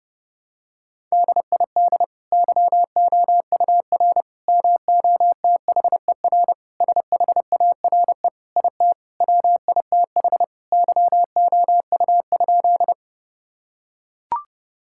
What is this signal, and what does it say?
Morse code, heard as a keyed tone, about 30 wpm, 700 Hz. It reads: DID YOUR MOTHER SHARE IT WITH YOU?
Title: Sentences from Top 1000 Words - Review Part 6 - 30wpm